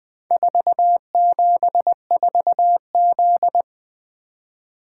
Morse code, heard as a keyed tone, about 20 wpm, 700 Hz. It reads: 474Z
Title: Four Letter-Number Combinations - 20wpm